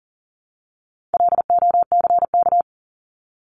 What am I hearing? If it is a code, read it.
Morse code, heard as a keyed tone, about 40 wpm, 700 Hz. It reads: LOCK
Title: Top 1001-1250 Words 40wpm